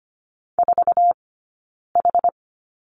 Morse code, heard as a keyed tone, about 25 wpm, 700 Hz. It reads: 4 H
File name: Lesson 15d - Number 5 - Two Characters 25wpm